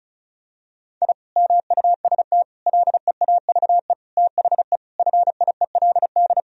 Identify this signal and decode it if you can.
Morse code, heard as a keyed tone, about 36 wpm, 700 Hz. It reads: I MUST LEAVE THE FIELD
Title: Sentences from Top 1000 Words - Review Part 2 - 35wpm